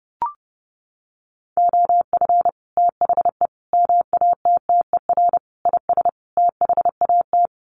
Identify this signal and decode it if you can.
Morse code, heard as a keyed tone, about 30 wpm, 700 Hz. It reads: OF THE MATTER IS THAT
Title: Sets of 5 Words Encore 30wpm